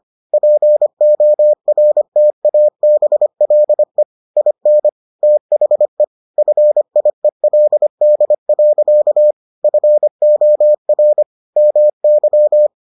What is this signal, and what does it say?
Morse code, heard as a keyed tone, about 25 wpm, 600 Hz. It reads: PORTABLE IN THE FIELD. FOR MY